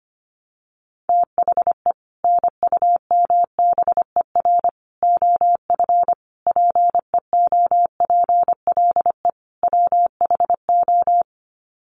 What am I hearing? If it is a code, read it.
Morse code, heard as a keyed tone, about 25 wpm, 700 Hz. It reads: THE NUMBER OF PEOPLE WHO
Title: Sets of 5 Words Encore 25wpm